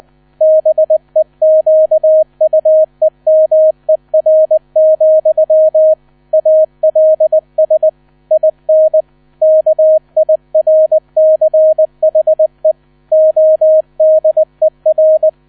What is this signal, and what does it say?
Morse code, heard as a keyed tone, about 20 wpm, 630 Hz. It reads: BEQUEMER, ALS IN KIRCHE ODER
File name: 000 Vorwort 20wpm